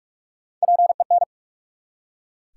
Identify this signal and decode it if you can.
Morse code, heard as a keyed tone, about 44 wpm, 700 Hz. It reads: PEN
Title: Three Letter Words - 45wpm